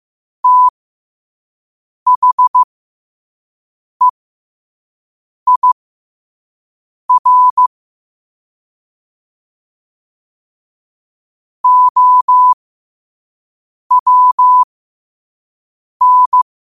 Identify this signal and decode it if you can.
Morse code, heard as a keyed tone, about 15 wpm, spaced as for 5 wpm, 1000 Hz. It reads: THEIR OWN
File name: qotdfiles / 5wpm